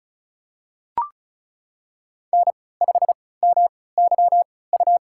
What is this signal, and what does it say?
Morse code, heard as a keyed tone, about 36 wpm, spaced as for 20 wpm, 700 Hz. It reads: N5MYU
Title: USA Call Signs - 5 Characters - 3x Character Spacing - 35wpm